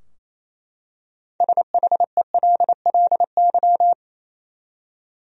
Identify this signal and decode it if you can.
Morse code, heard as a keyed tone, about 28 wpm, 700 Hz. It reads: SHELLY
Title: USA Common Names - 28wpm